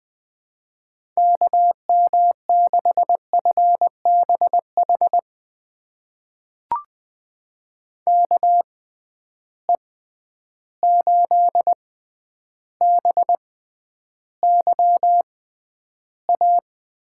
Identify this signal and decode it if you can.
Morse code, heard as a keyed tone, about 20 wpm, spaced as for 10 wpm, 700 Hz. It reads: KM6FBH KE8BYA
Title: USA Call Signs - 6 Characters - 6x Character Spacing - 20wpm